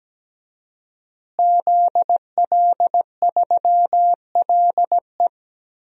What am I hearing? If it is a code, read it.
Morse code, heard as a keyed tone, about 17 wpm, 700 Hz. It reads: ZL3LE
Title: DXCC-100 Call Signs - 5 Characters - Repeated 2x - 17wpm